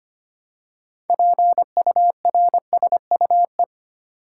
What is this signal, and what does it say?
Morse code, heard as a keyed tone, about 25 wpm, 700 Hz. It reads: PURSUE
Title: Six Letter Words - 25wpm